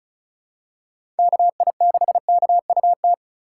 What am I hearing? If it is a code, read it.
Morse code, heard as a keyed tone, about 36 wpm, 700 Hz. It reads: KI6KUT
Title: USA Call Signs - 6 Characters - 3x Character Spacing - 35wpm